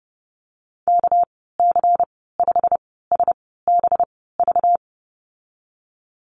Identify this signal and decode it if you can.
Morse code, heard as a keyed tone, about 30 wpm, spaced as for 17 wpm, 700 Hz. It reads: KC5SBV
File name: Lesson 37g - Letter x - Call Signs 30wpm